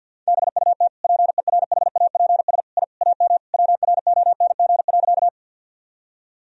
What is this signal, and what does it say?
Morse code, heard as a keyed tone, about 46 wpm, 700 Hz. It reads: BUT PERHAPS I AM WRONG.
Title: Sentences from Top 1000 Words - Review Part 2 - 50wpm